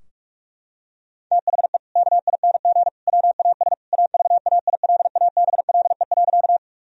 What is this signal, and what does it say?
Morse code, heard as a keyed tone, about 46 wpm, 700 Hz. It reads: THE KING WAS AVAILABLE.
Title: Sentences from Top 1000 Words - Review Part 3 - 45wpm